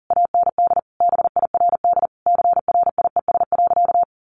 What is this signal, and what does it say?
Morse code, heard as a keyed tone, about 40 wpm, 700 Hz. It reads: AND BIRD CRIES.
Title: A Sound Of Thunder 40wpm